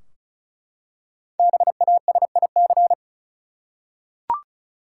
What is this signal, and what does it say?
Morse code, heard as a keyed tone, about 36 wpm, 700 Hz. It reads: BASIC
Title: Five Letter Words - 35wpm